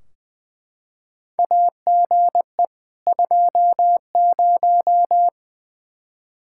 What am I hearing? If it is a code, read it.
Morse code, heard as a keyed tone, about 20 wpm, 700 Hz. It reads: AGE 20
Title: Practice QSO - Exchange 3 - More Info - 20wpm